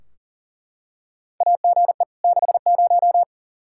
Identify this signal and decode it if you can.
Morse code, heard as a keyed tone, about 40 wpm, 690 Hz. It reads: AGE 60
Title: Practice QSO - Exchange 3 - More Info - 40wpm